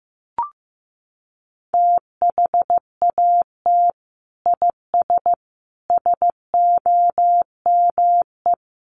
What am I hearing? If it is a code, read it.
Morse code, heard as a keyed tone, about 15 wpm, 700 Hz. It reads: THAT IS SOME
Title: Sentences from Top 1200 Words 15wpm